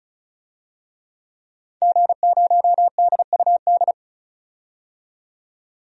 Morse code, heard as a keyed tone, about 36 wpm, 700 Hz. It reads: G0DUD
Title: DXCC-100 Call Signs - 5 Characters - 35wpm